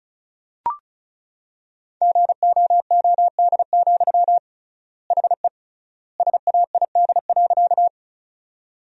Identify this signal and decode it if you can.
Morse code, heard as a keyed tone, about 36 wpm, 700 Hz. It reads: GOOD, HE SAID.
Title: 3x Word-Spacing - Sentences from Top 200 Words - 35wpm